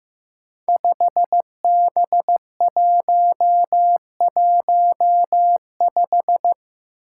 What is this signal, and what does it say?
Morse code, heard as a keyed tone, about 15 wpm, 700 Hz. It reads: 5B115
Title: Five Challenging Letter-Number Combinations 15wpm